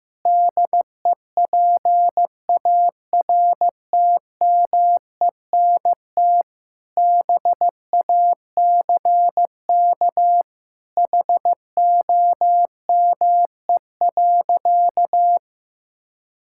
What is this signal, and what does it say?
Morse code, heard as a keyed tone, about 15 wpm, 700 Hz. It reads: DEPARTMENT BACK HOME.
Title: Sentences from Top 1000 Words - Review Part 6 - 15wpm